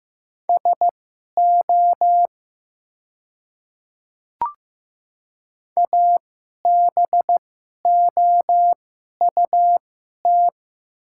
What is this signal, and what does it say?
Morse code, heard as a keyed tone, about 15 wpm, spaced as for 11 wpm, 700 Hz. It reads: SO ABOUT
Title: 2x Character Spacing - Top 100 Words - 15wpm